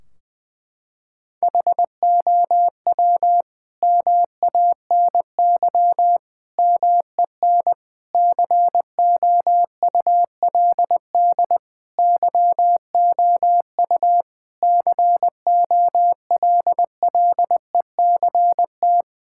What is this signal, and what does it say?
Morse code, heard as a keyed tone, about 20 wpm, 700 Hz. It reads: HOW MANY MEN COULD YOU COLLECT
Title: Sentences from Top 1600 Words 20wpm